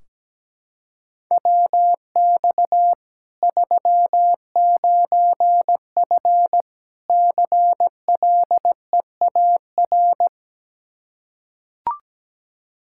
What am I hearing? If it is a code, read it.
Morse code, heard as a keyed tone, about 17 wpm, 700 Hz. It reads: WX 39F CLEAR
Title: Practice QSO - Exchange 2 - Info - 17wpm